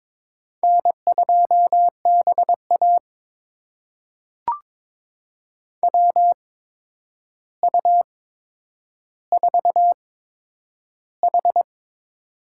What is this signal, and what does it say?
Morse code, heard as a keyed tone, about 22 wpm, spaced as for 9 wpm, 700 Hz. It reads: N2BA WU4H
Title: USA Call Signs - 4 Characters - 8x Character Spacing - 22wpm